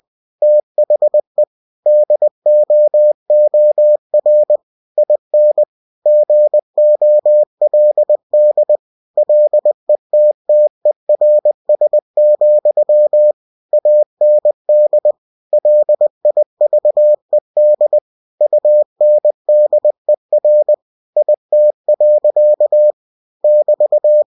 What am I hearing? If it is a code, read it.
Morse code, heard as a keyed tone, about 20 wpm, 600 Hz. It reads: THE DOOR IN GOLD LETTERS, AND LIVED UNDER IT. <BT>
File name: cw-books / winnie-the-pooh / winnie-20wpm-page-0002